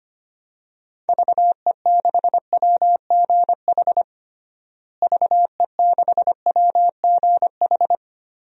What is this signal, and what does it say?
Morse code, heard as a keyed tone, about 25 wpm, 700 Hz. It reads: VE6WGH VE6WGH
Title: DXCC-100 Call Signs - 6 Characters - Repeated 2x - 25wpm